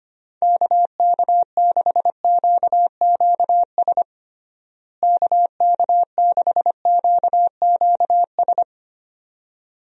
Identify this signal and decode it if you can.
Morse code, heard as a keyed tone, about 25 wpm, 700 Hz. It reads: KK6QQS KK6QQS
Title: USA Call Signs - 6 Characters - Repeated 2x - 25wpm